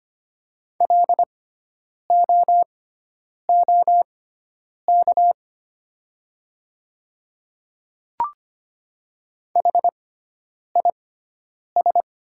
Morse code, heard as a keyed tone, about 25 wpm, spaced as for 9 wpm, 700 Hz. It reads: LOOK HIS
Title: 6x Character Spacing - Top 200 Words - 25wpm